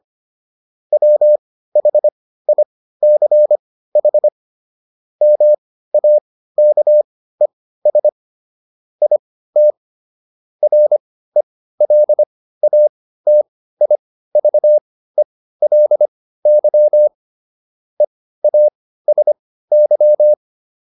Morse code, heard as a keyed tone, about 25 wpm, spaced as for 15 wpm, 600 Hz. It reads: WHICH MAKES IT RELATIVELY EASY